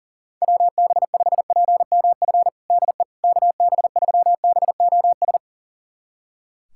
Morse code, heard as a keyed tone, about 40 wpm, 700 Hz. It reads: WB5PMF DE KB3BOS
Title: USA Callsign DE Callsign 40wpm